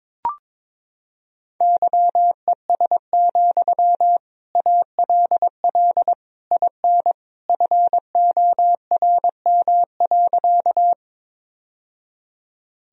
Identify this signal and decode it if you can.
Morse code, heard as a keyed tone, about 22 wpm, 700 Hz. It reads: YES, ALL IN FORM.
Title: Sentences from Top 1000 Words - Review Part 1 - 22wpm